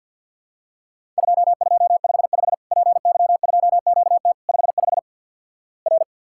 Morse code, heard as a keyed tone, about 48 wpm, 700 Hz. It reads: 2255 PY2QT 55 R
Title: Worked All Europe DX Contest - QTC Exchange - 50wpm